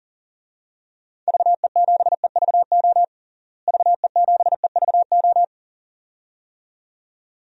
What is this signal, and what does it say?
Morse code, heard as a keyed tone, about 40 wpm, 700 Hz. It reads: VE7EVO VE7EVO
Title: DXCC-100 Call Signs - 6 Characters - Repeated 2x - 40wpm